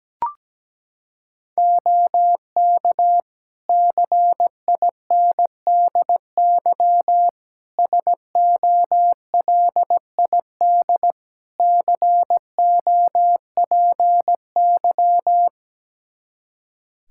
Morse code, heard as a keyed tone, about 17 wpm, 700 Hz. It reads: OK CINDY SOLID COPY
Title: K1USN SST QSO Practice 17wpm